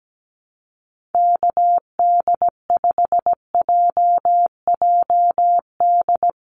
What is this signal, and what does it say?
Morse code, heard as a keyed tone, about 17 wpm, 700 Hz. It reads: KD5JJD
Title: USA Call Signs Repeated 3x 17wpm